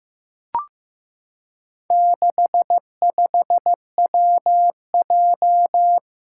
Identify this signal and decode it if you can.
Morse code, heard as a keyed tone, about 15 wpm, 700 Hz. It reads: 65WJ
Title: Four Challenging Letter-Number Combinations 15wpm